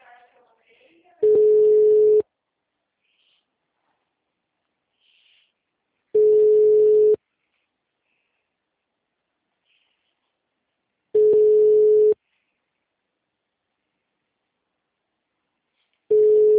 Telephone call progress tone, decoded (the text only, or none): ringback tone